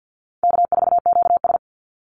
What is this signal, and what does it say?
Morse code, heard as a keyed tone, about 49 wpm, 700 Hz. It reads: K4QS
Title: NA SOTA Chaser Callsigns 50wpm